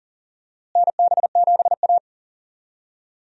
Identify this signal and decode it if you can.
Morse code, heard as a keyed tone, about 40 wpm, 700 Hz. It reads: NB7A
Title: USA Call Signs - 4 Characters - Repeated 2x - 40wpm